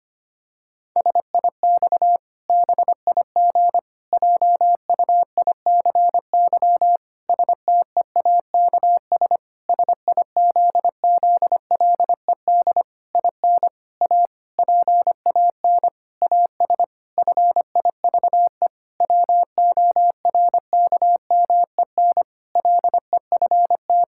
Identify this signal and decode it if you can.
Morse code, heard as a keyed tone, about 25 wpm, 700 Hz. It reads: SIX BIG JUICY STEAKS SIZZLED IN A PAN AS FIVE WORKMEN LEFT
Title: Pangram Challenge v2 - 25wpm